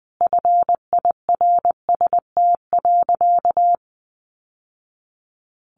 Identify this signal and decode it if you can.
Morse code, heard as a keyed tone, about 20 wpm, 700 Hz. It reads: FIRST.
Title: Sentences from Top 1200 Words 20wpm